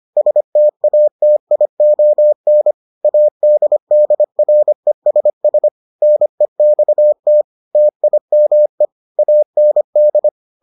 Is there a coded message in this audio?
Yes: Morse code from 0.2 to 10.3 s, heard as a keyed tone, about 25 wpm, 600 Hz. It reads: STATION ADDRESS NEXT TIME AND